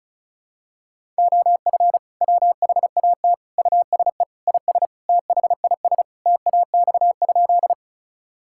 Morse code, heard as a keyed tone, about 36 wpm, 700 Hz. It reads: OF WHAT USE IS THIS TAX?
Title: Sentences from Top 1000 Words - Review Part 5 - 35wpm